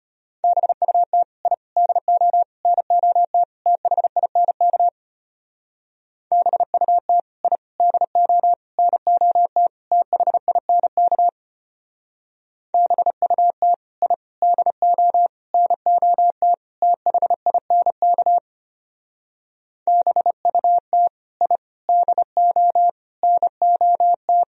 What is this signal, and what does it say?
Morse code, heard as a keyed tone, about 31 wpm, 700 Hz. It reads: BUT I DO NOT THINK BUT I DO NOT THINK BUT I DO NOT THINK BUT I DO NOT
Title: Speed-Racing - Sets of 5 Words 25wpm